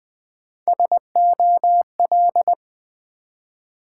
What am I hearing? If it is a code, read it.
Morse code, heard as a keyed tone, about 20 wpm, 700 Hz. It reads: SOL
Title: Build A Word - 1 Letter At A Time - 4-Letter Words - 20wpm